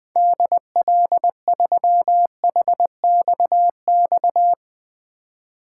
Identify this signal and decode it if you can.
Morse code, heard as a keyed tone, about 21 wpm, 700 Hz. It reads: DL3HXX
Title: 2023 SOTA Chaser Call Signs - 20wpm